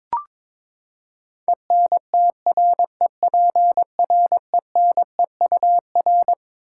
Morse code, heard as a keyed tone, about 22 wpm, 700 Hz. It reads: ENTREPRENEUR
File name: Twelve Letter Words - 22wpm